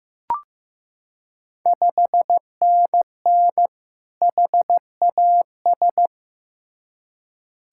Morse code, heard as a keyed tone, about 15 wpm, 700 Hz. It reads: 5NN HAS
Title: Ontario QSO Party Exchange 15wpm